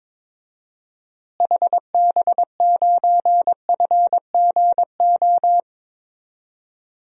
Morse code, heard as a keyed tone, about 22 wpm, 700 Hz. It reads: HB9FGO
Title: DXCC-100 Call Signs - 6 Characters - Repeated 2x - 22wpm